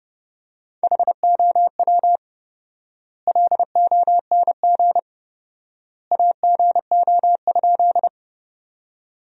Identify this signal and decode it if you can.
Morse code, heard as a keyed tone, about 30 wpm, 700 Hz. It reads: HOW LONG AGO?
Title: 4x Word-Spacing - Sentences from Top 500 Words - 30wpm